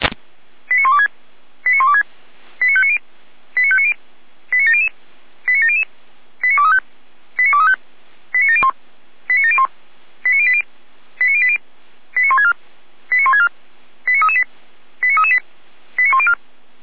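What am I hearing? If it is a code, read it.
Five-tone selective call (ZVEI), heard as quick runs of five tones, tones of about 80 ms. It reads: 89126 89126 89590 89590 89700 89700 89235 89235 89701 89701 89080 89080 89164 89164 89338 89338 89104
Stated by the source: ZVEI five-tone selective call received by SDR.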